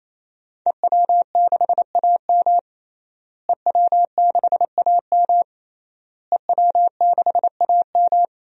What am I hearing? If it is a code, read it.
Morse code, heard as a keyed tone, about 28 wpm, 700 Hz. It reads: EW6AM EW6AM EW6AM
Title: DXCC-100 Call Signs - 5 Characters - Repeated 3x - 28wpm